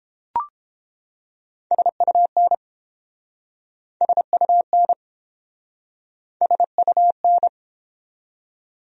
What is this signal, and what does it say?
Morse code, heard as a keyed tone, about 30 wpm, 700 Hz. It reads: SUN SUN SUN